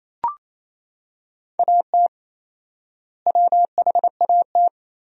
Morse code, heard as a keyed tone, about 28 wpm, 700 Hz. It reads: AT WHAT
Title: 4x Word-Spacing - Sentences from Top 100 Words - 28wpm